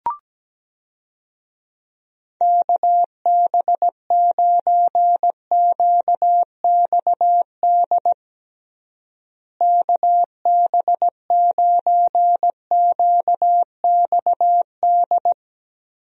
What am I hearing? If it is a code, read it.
Morse code, heard as a keyed tone, about 17 wpm, 700 Hz. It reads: KB9QXD KB9QXD
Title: USA Call Signs - 6 Characters - Repeated 3x - 17wpm